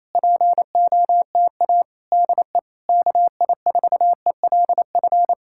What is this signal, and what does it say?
Morse code, heard as a keyed tone, about 26 wpm, 700 Hz. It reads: POTA DE KI4ELF
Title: Speed-Racing Common QSO Phrases - 28wpm